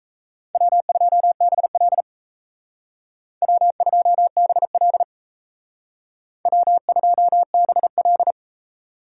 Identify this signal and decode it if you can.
Morse code, heard as a keyed tone, about 32 wpm, 700 Hz. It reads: W2BL W2BL W2BL
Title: Speed Racing - USA 1x2 Call Signs 28wpm